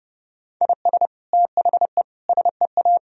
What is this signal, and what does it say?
Morse code, heard as a keyed tone, about 30 wpm, 700 Hz. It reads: IS THE SEA
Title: Sentences from Top 1000 Words - Review Part 4 - 30wpm